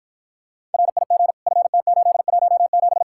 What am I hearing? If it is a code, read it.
Morse code, heard as a keyed tone, about 52 wpm, 700 Hz. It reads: RIG FT817